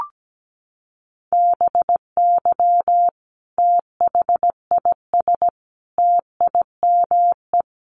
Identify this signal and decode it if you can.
Morse code, heard as a keyed tone, about 17 wpm, 700 Hz. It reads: BY THIS TIME
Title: Sentences from Top 1400 Words 17wpm